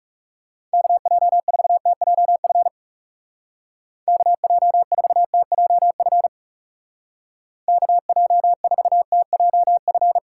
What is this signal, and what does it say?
Morse code, heard as a keyed tone, about 36 wpm, 700 Hz. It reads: KJ4TJF KJ4TJF KJ4TJF
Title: Speed Racing - USA Call Signs 30wpm